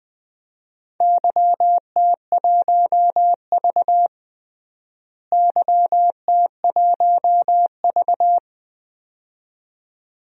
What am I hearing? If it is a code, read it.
Morse code, heard as a keyed tone, about 20 wpm, 700 Hz. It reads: YT1V YT1V
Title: DXCC-100 Call Signs - 4 Characters - Repeated 2x - 20wpm